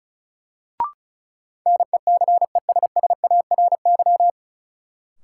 Morse code, heard as a keyed tone, about 34 wpm, 700 Hz. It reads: NECESSARY
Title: Sentences from Top 800 Words - New Word Drill - 35wpm